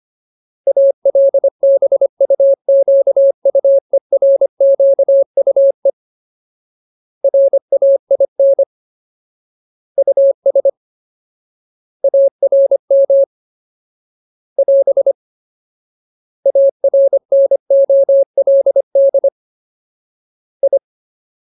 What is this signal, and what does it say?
Morse code, heard as a keyed tone, about 25 wpm, 550 Hz. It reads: ALBUQUERQUE RAIN US ARM <AS> ARNOLD I
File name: Ultimate Word List / 25WPM Ultimate